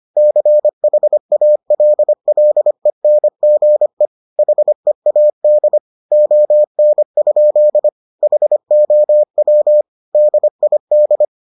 Morse code, heard as a keyed tone, about 25 wpm, 600 Hz. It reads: CHALLENGE HEAD ON? HOW DID